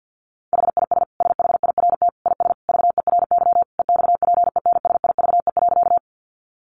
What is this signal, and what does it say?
Morse code, heard as a keyed tone, about 50 wpm, 700 Hz. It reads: HIS SHIRT IS VERY EXPENSIVE.
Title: Additional Easy Phrases 50wpm